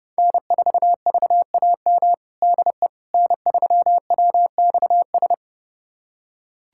Morse code, heard as a keyed tone, about 30 wpm, 700 Hz. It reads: N4VAM DE N3WXS
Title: USA Callsign DE Callsign - 5 Characters - 30wpm